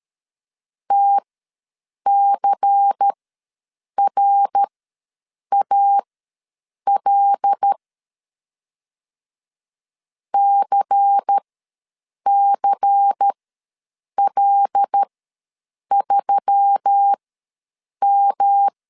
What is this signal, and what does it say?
Morse code, heard as a keyed tone, about 13 wpm, spaced as for 7 wpm, 800 Hz. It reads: TCRAL CCL3M
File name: LEZIONI / 5 LBVC / 5 LBVC ws 12wpm 3159